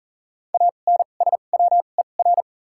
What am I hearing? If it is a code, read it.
Morse code, heard as a keyed tone, about 40 wpm, spaced as for 29 wpm, 700 Hz. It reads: ANSWER
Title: 2x Character Spacing - Top 300 Words - 40wpm